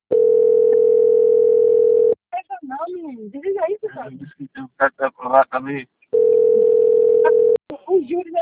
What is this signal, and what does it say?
Ringback tone (North America), 440 + 480 Hz, with speech.
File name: out- - - - - .4346